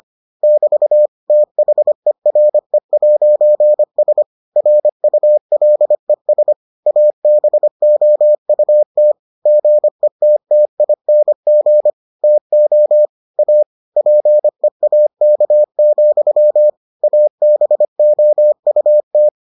Morse code, heard as a keyed tone, about 25 wpm, 600 Hz. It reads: <BT> THERE'S RULES ABOUT GETTING TO A PEAK, ABOUT